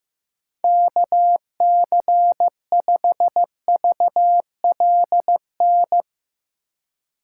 Morse code, heard as a keyed tone, about 16 wpm, 700 Hz. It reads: KC5VLN
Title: USA Call Signs - 6 Characters - 2x Character Spacing - 15wpm